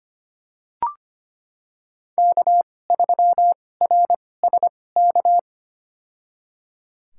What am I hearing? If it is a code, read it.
Morse code, heard as a keyed tone, about 25 wpm, spaced as for 18 wpm, 700 Hz. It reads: K3RSK
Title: USA Call Signs - 5 Characters - 2x Character Spacing - 25wpm